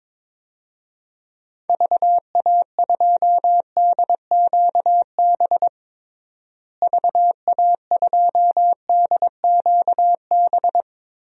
Morse code, heard as a keyed tone, about 22 wpm, 700 Hz. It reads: VA2DQB VA2DQB
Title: DXCC-100 Call Signs - 6 Characters - Repeated 3x - 22wpm